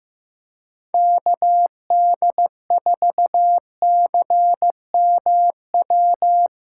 Morse code, heard as a keyed tone, about 15 wpm, 700 Hz. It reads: KD4CMW
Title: USA Callsign DE Callsign 15wpm